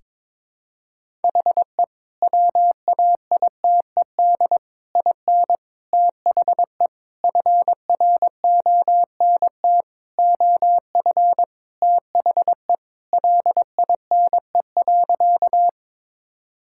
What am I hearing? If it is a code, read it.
Morse code, heard as a keyed tone, about 22 wpm, 700 Hz. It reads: HE WAITED IN THE FRONT OF THE LINE.
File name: practice / 22wpm / Sentences with 4-Word Phrases - 22wpm